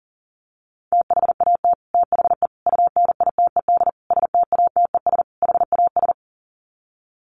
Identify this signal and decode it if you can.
Morse code, heard as a keyed tone, about 40 wpm, 700 Hz. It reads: THAT THE UNITED STATES HAS
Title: Sets of 5 Words Encore 40wpm